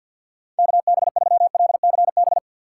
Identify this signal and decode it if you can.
Morse code, heard as a keyed tone, about 53 wpm, 700 Hz. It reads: KB3LCB
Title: USA Call Signs - 6 Characters - 8x Character Spacing - 50wpm